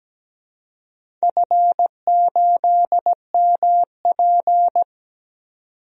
Morse code, heard as a keyed tone, about 17 wpm, 700 Hz. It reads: F8MP
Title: DXCC-100 Call Signs - 4 Characters - 17wpm